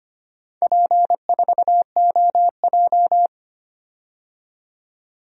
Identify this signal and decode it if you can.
Morse code, heard as a keyed tone, about 25 wpm, 690 Hz. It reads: P4OJ
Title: DXCC-100 Call Signs - 4 Characters - 25wpm